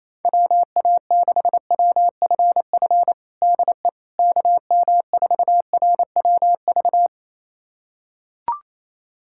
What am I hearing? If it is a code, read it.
Morse code, heard as a keyed tone, about 28 wpm, 700 Hz. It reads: WA6WFF DE KM4RWV
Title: USA Callsign DE Callsign 28wpm